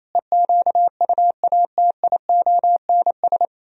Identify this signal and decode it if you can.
Morse code, heard as a keyed tone, about 38 wpm, 700 Hz. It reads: EQUATIONS